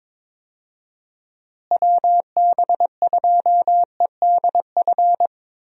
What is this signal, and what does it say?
Morse code, heard as a keyed tone, about 22 wpm, 700 Hz. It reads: WB2EDF
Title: USA Call Signs - 6 Characters - Repeated 2x - 22wpm